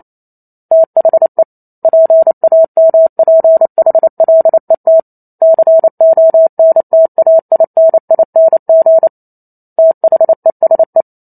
Morse code, heard as a keyed tone, about 28 wpm, 650 Hz. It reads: THE PAMPHLET CONTAINING THESE